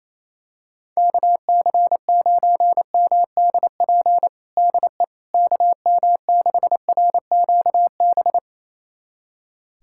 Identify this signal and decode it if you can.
Morse code, heard as a keyed tone, about 28 wpm, 700 Hz. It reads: KC9MDP DE KM6RQB
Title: USA Callsign DE Callsign - 6 Characters - 28wpm